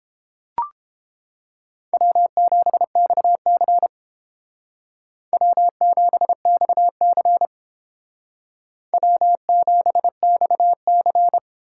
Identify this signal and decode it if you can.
Morse code, heard as a keyed tone, about 30 wpm, 700 Hz. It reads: W7XC W7XC W7XC